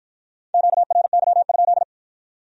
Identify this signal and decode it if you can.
Morse code, heard as a keyed tone, about 53 wpm, 700 Hz. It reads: QRY?